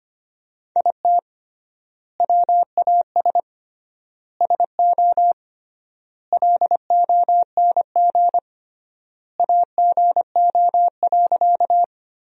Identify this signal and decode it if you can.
Morse code, heard as a keyed tone, about 25 wpm, 700 Hz. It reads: IT WAS SO LONG AGO.
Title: 3x Word-Spacing - Sentences from Top 500 Words - 25wpm